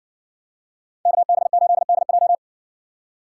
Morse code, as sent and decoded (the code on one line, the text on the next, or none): -.- -... ---.. -.. .---
KB8DJ